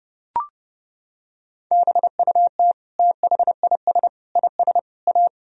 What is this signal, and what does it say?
Morse code, heard as a keyed tone, about 32 wpm, 700 Hz. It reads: BUT THIS IS A
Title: Sentences from Top 1000 Words - Review Part 3 - 30wpm